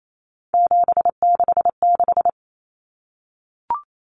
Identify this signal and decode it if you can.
Morse code, heard as a keyed tone, about 28 wpm, 700 Hz. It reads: 766
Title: Common Cut Numbers - 3-Digit 28wpm